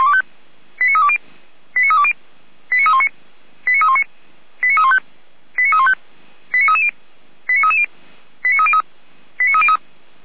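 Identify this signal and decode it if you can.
Five-tone selective call (ZVEI), heard as quick runs of five tones, tones of about 80 ms. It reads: 89320 89320 89319 89319 89315 89315 89330 89330 89303 89303
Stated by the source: ZVEI five-tone call received by SDR.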